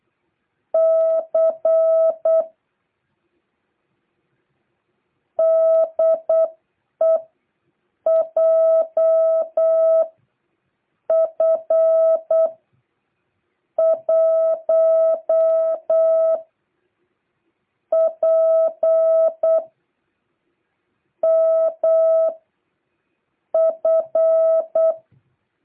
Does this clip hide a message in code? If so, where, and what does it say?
Yes: Morse code from 0.7 to 24.9 s, heard as a keyed tone, about 8 wpm, spaced as for 5 wpm, 660 Hz. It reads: C DE JF1PMF